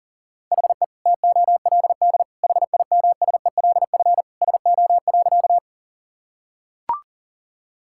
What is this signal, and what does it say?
Morse code, heard as a keyed tone, about 37 wpm, 700 Hz. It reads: HE TOLD HIMSELF SO.
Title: Sentences from Top 1600 Words 40wpm